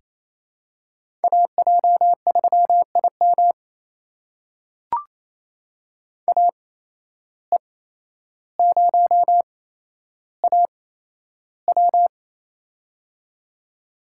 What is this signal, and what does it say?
Morse code, heard as a keyed tone, about 28 wpm, spaced as for 12 wpm, 700 Hz. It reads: AJ3IM AE0AW
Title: USA Call Signs - 5 Characters - 8x Character Spacing - 28wpm